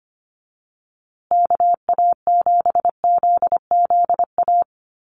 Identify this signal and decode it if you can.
Morse code, heard as a keyed tone, about 25 wpm, 700 Hz. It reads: KA7ZZA
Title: USA Call Signs Repeated 3x 25wpm